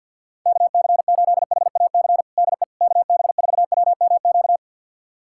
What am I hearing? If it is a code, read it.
Morse code, heard as a keyed tone, about 47 wpm, 700 Hz. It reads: KC8HAC DE KB4WMQ
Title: USA Callsign DE Callsign - 6 Characters - 50wpm